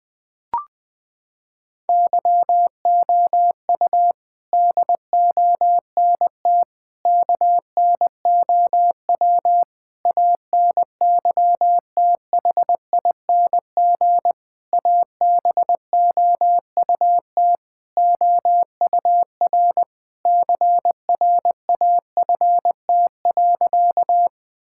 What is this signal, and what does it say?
Morse code, heard as a keyed tone, about 20 wpm, 700 Hz. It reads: YOU DONT KNOW ANYTHING ABOUT OUR CRAFT.
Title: Sentences from Top 1000 Words - Review Part 6 - 20wpm